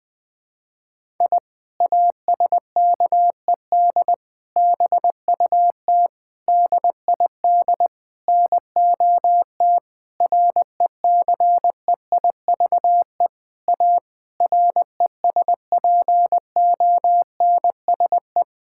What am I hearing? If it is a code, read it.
Morse code, heard as a keyed tone, about 20 wpm, 700 Hz. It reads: I ASKED BUT DID NOT RECEIVE A RESPONSE
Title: Sentences with 8-Word Phrases - 20wpm